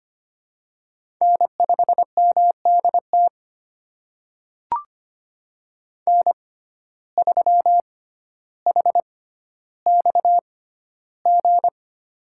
Morse code, heard as a keyed tone, about 25 wpm, spaced as for 13 wpm, 700 Hz. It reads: N5MDT N3HXG
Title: USA Call Signs - 5 Characters - 6x Character Spacing - 25wpm